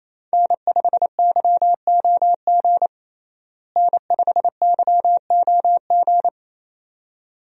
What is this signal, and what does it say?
Morse code, heard as a keyed tone, about 28 wpm, 700 Hz. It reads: N5YOG N5YOG
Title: USA Call Signs - 5 Characters - Repeated 2x - 28wpm